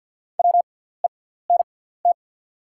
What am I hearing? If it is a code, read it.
Morse code, heard as a keyed tone, about 48 wpm, spaced as for 17 wpm, 690 Hz. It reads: WENT
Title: 6x Character Spacing - Top 200 Words - 50wpm